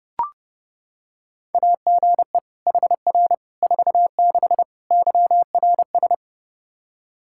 Morse code, heard as a keyed tone, about 32 wpm, 700 Hz. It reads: AGE HR 46 YRS
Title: Practice QSO - Exchange 3 - More Info - 30wpm